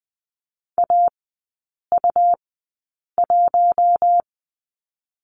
Morse code, heard as a keyed tone, about 20 wpm, 700 Hz. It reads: A U 1
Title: 100 Lesson 17e - Letter u - Three Characters 20wpm